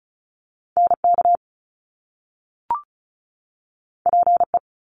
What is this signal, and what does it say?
Morse code, heard as a keyed tone, about 36 wpm, 700 Hz. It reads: NK PE
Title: Top Two Letter Combinations in Words 35wpm